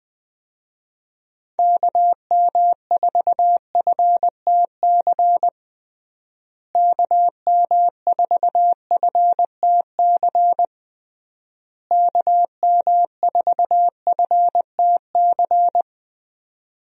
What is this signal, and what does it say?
Morse code, heard as a keyed tone, about 20 wpm, 700 Hz. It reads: KM4FTC KM4FTC KM4FTC
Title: USA Call Signs - 6 Characters - Repeated 3x - 20wpm